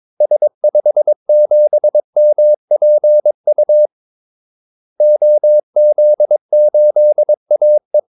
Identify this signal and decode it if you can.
Morse code, heard as a keyed tone, about 22 wpm, 600 Hz. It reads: S57MPU OZ8AE
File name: Callsigns-22wpm